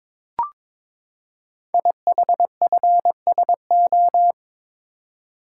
Morse code, heard as a keyed tone, about 22 wpm, 700 Hz. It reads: IHFSO
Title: Code Groups - 5 Letters - 22wpm